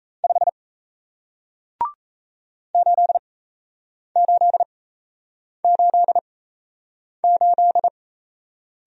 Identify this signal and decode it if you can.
Morse code, heard as a keyed tone, about 36 wpm, spaced as for 8 wpm, 700 Hz. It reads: 5 8888